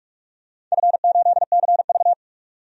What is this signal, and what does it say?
Morse code, heard as a keyed tone, about 45 wpm, 700 Hz. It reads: F8CV
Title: DXCC-100 Call Signs - 4 Characters - 45wpm